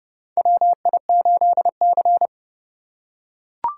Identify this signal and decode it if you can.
Morse code, heard as a keyed tone, about 30 wpm, 700 Hz. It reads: WI8C
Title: USA Call Signs - 4 Characters - 4x Character Spacing - 30wpm